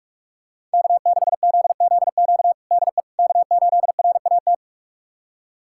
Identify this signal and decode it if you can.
Morse code, heard as a keyed tone, about 46 wpm, 700 Hz. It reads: K6ZZQ DE K8RAT
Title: USA Callsign DE Callsign 45wpm